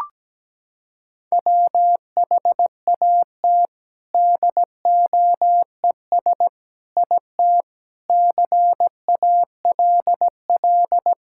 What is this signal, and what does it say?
Morse code, heard as a keyed tone, about 17 wpm, 700 Hz. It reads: WHAT DOES IT CALL
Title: Sentences from Top 1000 Words - Review Part 5 - 17wpm